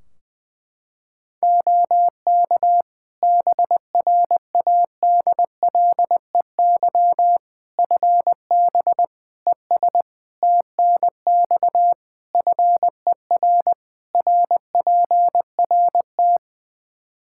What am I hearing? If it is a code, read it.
Morse code, heard as a keyed tone, about 20 wpm, 700 Hz. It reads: OK BRADLEY FB ES TNX FER RPRT
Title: Practice QSO - Exchange 2 - Info - 20wpm